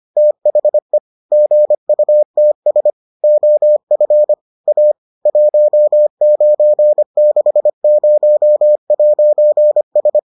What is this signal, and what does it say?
Morse code, heard as a keyed tone, about 25 wpm, 600 Hz. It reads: THE GUTS OF A 1960'S